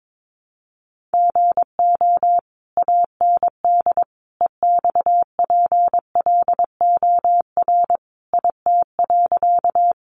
Morse code, heard as a keyed tone, about 22 wpm, 700 Hz. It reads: GO AND EXPLOR IT.
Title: Sentences from Top 1100 Words 22wpm